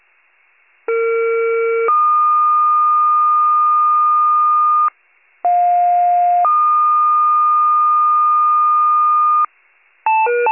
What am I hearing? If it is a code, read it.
Two-tone page, heard as a one-second tone, then a three-second tone, about 460 then 1220 Hz, about 710 then 1220 Hz. It is heard from 0.9 to 9.5 s.